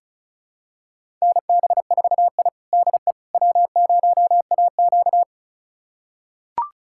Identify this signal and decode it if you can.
Morse code, heard as a keyed tone, about 34 wpm, 700 Hz. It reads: NB4I DE W0AQ
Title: USA Callsign DE Callsign - 4 Characters - 35wpm